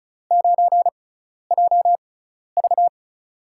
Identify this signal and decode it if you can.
Morse code, heard as a keyed tone, about 36 wpm, spaced as for 12 wpm, 700 Hz. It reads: 9JV
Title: USA Call Signs - 5 Characters - 6x Character Spacing - 35wpm